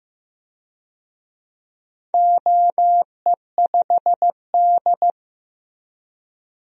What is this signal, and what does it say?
Morse code, heard as a keyed tone, about 15 wpm, 700 Hz. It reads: OE5D
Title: DXCC-100 Call Signs - 4 Characters - Repeated 3x - 15wpm